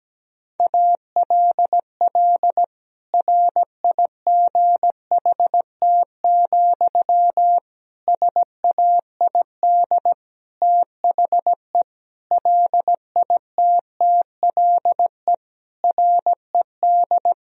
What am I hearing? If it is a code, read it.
Morse code, heard as a keyed tone, about 17 wpm, 700 Hz. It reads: ALL RIGHT, SAID THE LITTLE RED
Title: Sentences from Top 1000 Words - Review Part 2 - 17wpm